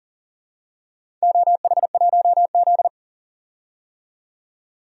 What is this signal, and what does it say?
Morse code, heard as a keyed tone, about 42 wpm, 690 Hz. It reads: OH1Z
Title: DXCC-100 Call Signs - 4 Characters - 40wpm